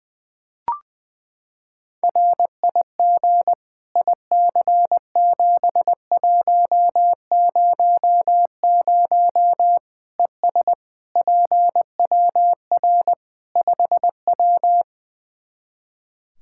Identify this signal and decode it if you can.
Morse code, heard as a keyed tone, about 20 wpm, 700 Hz. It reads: RIG IC7100 ES PWR 5W
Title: Practice QSO - Exchange 2 - Info - 20wpm